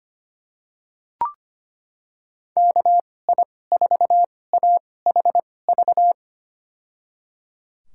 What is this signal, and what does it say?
Morse code, heard as a keyed tone, about 25 wpm, spaced as for 18 wpm, 700 Hz. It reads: KI4AHV
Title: USA Call Signs - 6 Characters - 2x Character Spacing - 25wpm